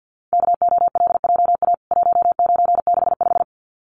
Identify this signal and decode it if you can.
Morse code, heard as a keyed tone, about 50 wpm, 700 Hz. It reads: KOLJA 1965
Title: CWT Exchange 50wpm